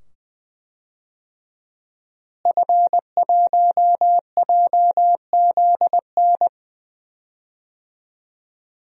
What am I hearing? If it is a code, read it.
Morse code, heard as a keyed tone, about 20 wpm, 700 Hz. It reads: F1JZN
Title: DXCC-100 Call Signs - 5 Characters - Repeated 2x - 20wpm